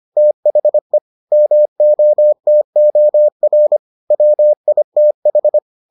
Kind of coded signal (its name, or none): Morse code